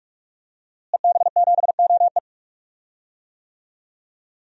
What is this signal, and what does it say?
Morse code, heard as a keyed tone, about 42 wpm, 700 Hz. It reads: EB7OE
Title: DXCC-100 Call Signs - 5 Characters - Repeated 2x - 45wpm